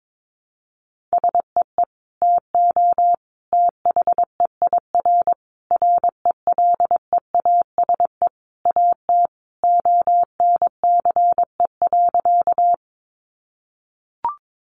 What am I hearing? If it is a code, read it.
Morse code, heard as a keyed tone, about 22 wpm, 700 Hz. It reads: SEE TO THEIR RELEASE AT ONCE.
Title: Sentences from Top 1100 Words 22wpm